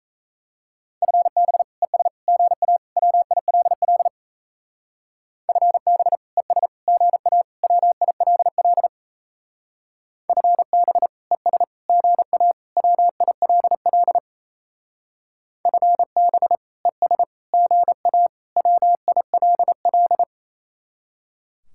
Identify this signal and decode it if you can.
Morse code, heard as a keyed tone, about 37 wpm, 700 Hz. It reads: FB ES GA WILL FB ES GA WILL FB ES GA WILL FB ES GA WILL